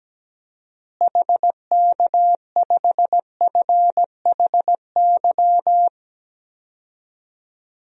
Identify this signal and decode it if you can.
Morse code, heard as a keyed tone, about 17 wpm, 700 Hz. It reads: HK5FHY
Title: DXCC-100 Call Signs - 6 Characters - Repeated 2x - 17wpm